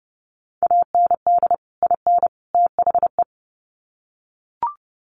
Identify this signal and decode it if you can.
Morse code, heard as a keyed tone, about 30 wpm, 700 Hz. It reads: AND IN THE
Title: Sets of 3 Words 30wpm